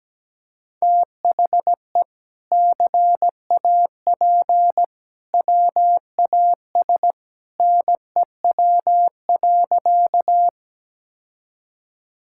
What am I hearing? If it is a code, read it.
Morse code, heard as a keyed tone, about 17 wpm, 700 Hz. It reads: THE CAP WAS NEW.
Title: Sentences from Top 1600 Words 17wpm